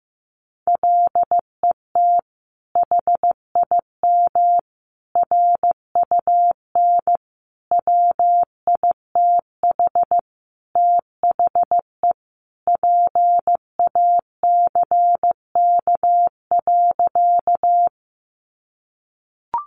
Morse code, heard as a keyed tone, about 15 wpm, 700 Hz. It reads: LET HIM RUN WITH THE PACK.
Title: Sentences from Top 1300 Words 15wpm